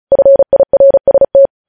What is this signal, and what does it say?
Morse code, heard as a keyed tone, about 36 wpm, 550 Hz. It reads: FIRST
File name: podcast / News headlines-1x-35x35